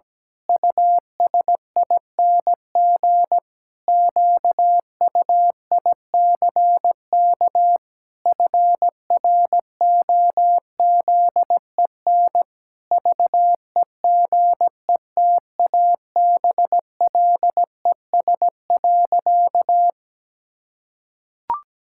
Morse code, heard as a keyed tone, about 17 wpm, 700 Hz. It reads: USING QUICK FROZEN VEGETABLES.
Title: Pangram Challenge v2 - 17wpm